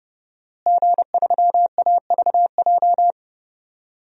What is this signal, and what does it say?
Morse code, heard as a keyed tone, about 30 wpm, 700 Hz. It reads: G3AVJ
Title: DXCC-100 Call Signs - 5 Characters - Repeated 2x - 30wpm